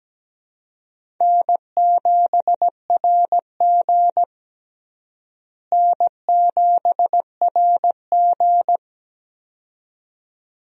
Morse code, heard as a keyed tone, about 17 wpm, 700 Hz. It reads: N7RG N7RG
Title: USA Call Signs - 4 Characters - Repeated 2x - 17wpm